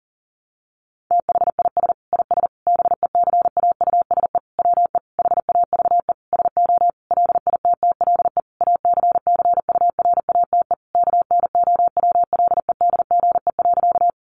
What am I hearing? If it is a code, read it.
Morse code, heard as a keyed tone, about 40 wpm, 700 Hz. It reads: THIS IS BECAUSE WE HAVE SO LITTLE ACCURATE KNOWLEDGE.
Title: Sentences from Top 1200 Words 40wpm